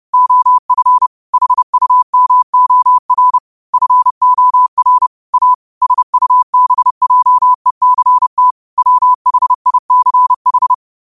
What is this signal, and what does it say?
Morse code, heard as a keyed tone, about 30 wpm, 1000 Hz. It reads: OF HUMOR FOR A SUBJECT WHICH
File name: 30wpm